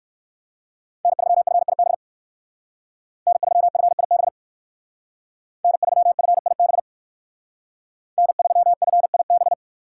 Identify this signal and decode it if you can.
Morse code, heard as a keyed tone, about 59 wpm, 700 Hz. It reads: N3FIB N3FIB N3FIB N3FIB